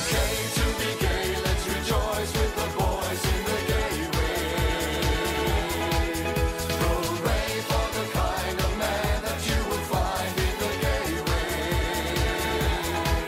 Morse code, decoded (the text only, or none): none